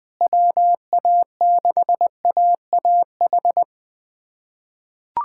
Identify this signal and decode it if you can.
Morse code, heard as a keyed tone, about 21 wpm, 700 Hz. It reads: WA6AAH
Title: USA Callsign DE Callsign 20wpm